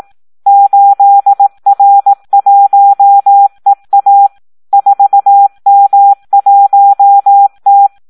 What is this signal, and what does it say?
Morse code, heard as a keyed tone, about 18 wpm, 800 Hz. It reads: 8R1EA 4M1T
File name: Audio practica CW / SD 1 18